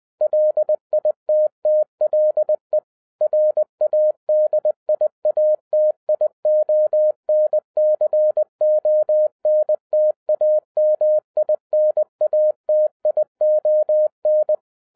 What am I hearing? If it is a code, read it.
Morse code, heard as a keyed tone, about 20 wpm, 610 Hz. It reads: LITTLE RADIATION CONTAMINATION